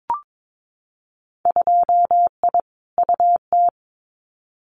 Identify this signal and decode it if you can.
Morse code, heard as a keyed tone, about 22 wpm, 700 Hz. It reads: 2I UT
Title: Winter Field Day Exchanges 22wpm